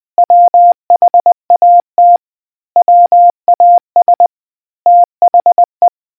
Morse code, heard as a keyed tone, about 20 wpm, 700 Hz. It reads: WHAT WAS THE